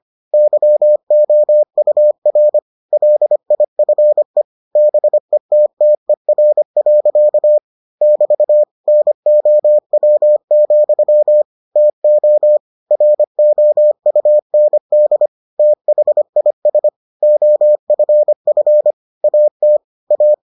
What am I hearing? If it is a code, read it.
Morse code, heard as a keyed tone, about 25 wpm, 600 Hz. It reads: YOUR LIFE BETTER. <BT> NOW, TO ROUND THIS OFF AT A